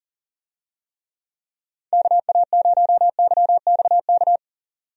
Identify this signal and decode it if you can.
Morse code, heard as a keyed tone, about 40 wpm, 700 Hz. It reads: KA0YXK
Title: USA Call Signs - 6 Characters - 40wpm